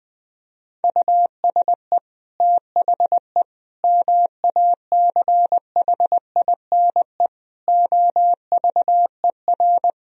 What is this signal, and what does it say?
Morse code, heard as a keyed tone, about 20 wpm, 700 Hz. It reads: USE THE MACHINE OVER
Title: Sentences from Top 1000 Words - Review Part 2 - 20wpm